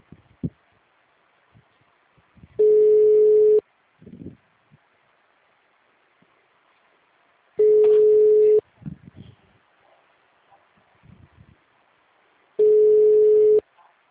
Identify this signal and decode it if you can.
Ringback tone (Europe, 425 Hz), with speech.